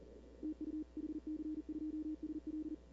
Morse code, heard as a keyed tone, about 40 wpm, 310 Hz. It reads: TU HC2SL
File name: CW / 2005-k5tr arrl cw dx -19:00